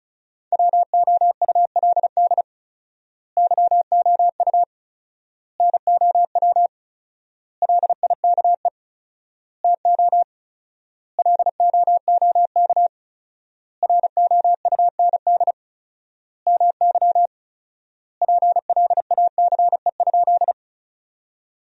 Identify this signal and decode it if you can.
Morse code, heard as a keyed tone, about 35 wpm, 700 Hz. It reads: WOULD YOU NOW LIKE TO LOOK ROUND MY PLACE?
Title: 4x Word-Spacing - Sentences from Top 200 Words - 35wpm